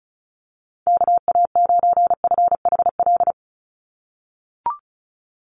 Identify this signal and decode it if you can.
Morse code, heard as a keyed tone, about 34 wpm, 700 Hz. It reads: KA9FHL
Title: USA Call Signs 35wpm